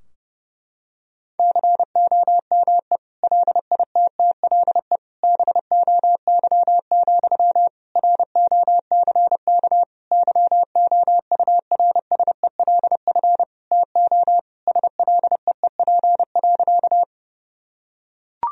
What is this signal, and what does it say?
Morse code, heard as a keyed tone, about 32 wpm, 700 Hz. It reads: COME LITTLE BOY, ROCK YOURSELF TO SLEEP.
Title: Sentences from Top 1000 Words - Review Part 2 - 30wpm